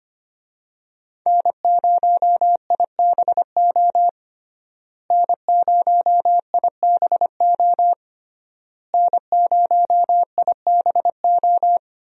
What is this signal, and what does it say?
Morse code, heard as a keyed tone, about 25 wpm, 700 Hz. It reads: N0IBO N0IBO N0IBO
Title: USA Call Signs - 5 Characters - Repeated 3x - 25wpm